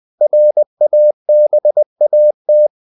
Morse code, heard as a keyed tone, about 20 wpm, 600 Hz. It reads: RABAT
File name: CW / Practice / NationalCapitals20